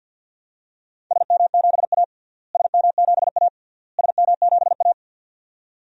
Morse code, heard as a keyed tone, about 49 wpm, 700 Hz. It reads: SM7A SM7A SM7A
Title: DXCC-100 Call Signs - 4 Characters - Repeated 3x - 50wpm